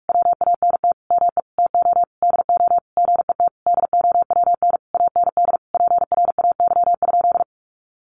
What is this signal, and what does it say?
Morse code, heard as a keyed tone, about 44 wpm, 700 Hz. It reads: WANT ME TO DO GET DOWN AND PRAY?
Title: A Sound Of Thunder 45wpm